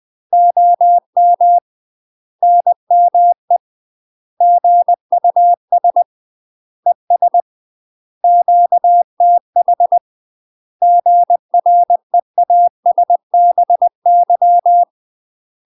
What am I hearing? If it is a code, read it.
Morse code, heard as a keyed tone, about 20 wpm, 700 Hz. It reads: OM NME GUS ES QTH GREASBY
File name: UK MASTER QSOs TRAINING/20WPM